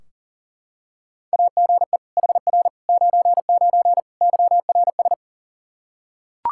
Morse code, heard as a keyed tone, about 40 wpm, 700 Hz. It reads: AGE HR 99 YRS
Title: Practice QSO - Exchange 3 - More Info - 40wpm